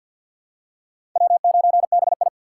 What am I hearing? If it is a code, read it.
Morse code, heard as a keyed tone, about 50 wpm, 700 Hz. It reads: W9BI
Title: USA Call Signs - 4 Characters - 3x Character Spacing - 50wpm